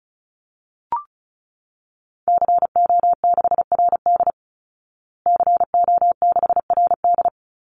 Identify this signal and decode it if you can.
Morse code, heard as a keyed tone, about 34 wpm, 700 Hz. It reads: CO6RD CO6RD
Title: Call Signs Repeated 2x - 35wpm